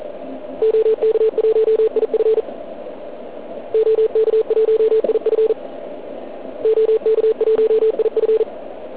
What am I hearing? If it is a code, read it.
Morse code, heard as a keyed tone, about 40 wpm, 440 Hz, with radio noise. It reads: OK1IF OK1IF OK1IF